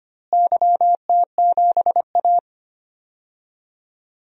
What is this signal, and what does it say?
Morse code, heard as a keyed tone, about 25 wpm, 700 Hz. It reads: YT7A
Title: DXCC-100 Call Signs - 4 Characters - Repeated 3x - 25wpm